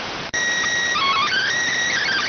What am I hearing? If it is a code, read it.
SSTV (slow-scan television), PD 290.